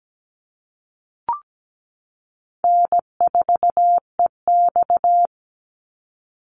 Morse code, heard as a keyed tone, about 17 wpm, 700 Hz. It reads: N4EX
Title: NA SOTA Chaser Callsigns 17wpm